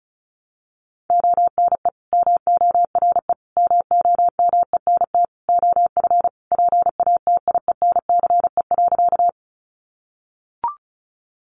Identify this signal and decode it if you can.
Morse code, heard as a keyed tone, about 34 wpm, 700 Hz. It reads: ONE MORE MOMENT OF PATIENCE.
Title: Sentences from Top 1300 Words 35wpm